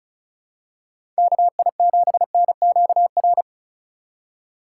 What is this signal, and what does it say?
Morse code, heard as a keyed tone, about 36 wpm, 700 Hz. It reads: KI7NQR
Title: USA Call Signs - 6 Characters - 8x Character Spacing - 35wpm